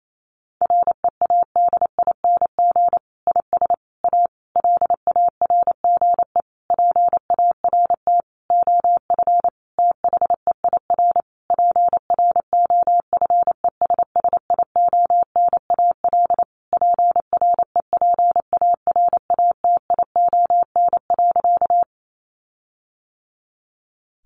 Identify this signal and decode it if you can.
Morse code, heard as a keyed tone, about 28 wpm, 700 Hz. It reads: READING IS A LARGE PART OF THEIR PROFESSIONAL PREPARATION.
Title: Sentences from Top 1400 Words 28wpm